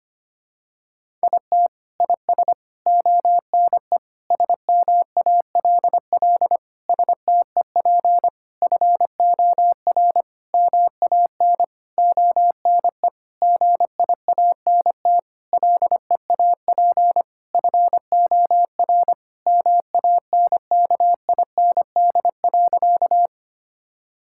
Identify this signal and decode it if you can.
Morse code, heard as a keyed tone, about 25 wpm, 700 Hz. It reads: IT IS ONE SMALL STEP FOR MAN ONE GIANT LEAP FOR MANKIND.
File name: practice / 25wpm / Sentences with 9-Word Phrases - 25wpm